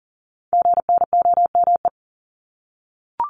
Morse code, heard as a keyed tone, about 40 wpm, 700 Hz. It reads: GNOME
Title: Multi-letter Phonemes 40wpm